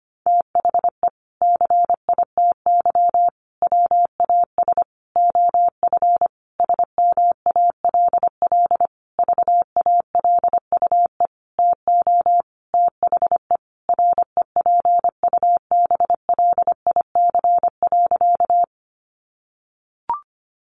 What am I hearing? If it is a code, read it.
Morse code, heard as a keyed tone, about 25 wpm, 700 Hz. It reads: THE CITY WAS OF SMALL VALUE TO THE REPUBLIC.
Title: Sentences from Top 1500 Words 25wpm